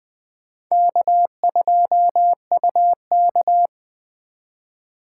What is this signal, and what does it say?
Morse code, heard as a keyed tone, about 21 wpm, 700 Hz. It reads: K2UK
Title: Speed Racing - USA 1x2 Call Signs 17wpm